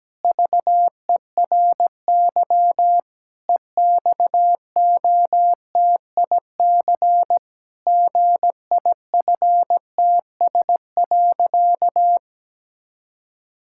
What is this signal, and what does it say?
Morse code, heard as a keyed tone, about 17 wpm, 700 Hz. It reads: VERY EXOTIC GIFTS.
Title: Pangram Challenge v2 - 17wpm